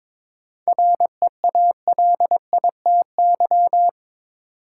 Morse code, heard as a keyed tone, about 22 wpm, 700 Hz. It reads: REALITY
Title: Sentences from Top 1200 Words - New Word Drill Alt - 22wpm